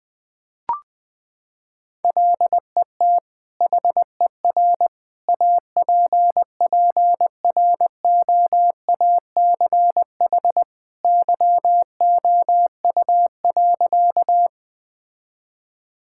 Morse code, heard as a keyed tone, about 20 wpm, 700 Hz. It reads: LET HER APPROACH YOU.
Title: Sentences from Top 1000 Words - Review Part 6 - 20wpm